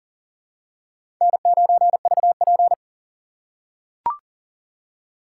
Morse code, heard as a keyed tone, about 38 wpm, 700 Hz. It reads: N9VP
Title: USA Call Signs - 4 Characters - 2x Character Spacing - 40wpm